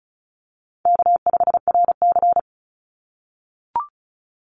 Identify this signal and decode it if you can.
Morse code, heard as a keyed tone, about 34 wpm, 700 Hz. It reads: K5RC
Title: Call Signs - 35wpm